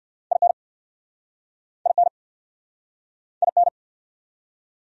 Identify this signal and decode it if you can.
Morse code, heard as a keyed tone, about 59 wpm, 700 Hz. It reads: IN IN IN
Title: Speed-Racing Top 100 Words 50wpm